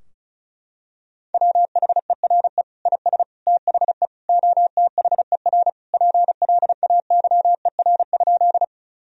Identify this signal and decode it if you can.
Morse code, heard as a keyed tone, about 34 wpm, 700 Hz. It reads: WHERE IS THE OTHER PLAYER?
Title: Sentences from Top 1000 Words - Review Part 4 - 35wpm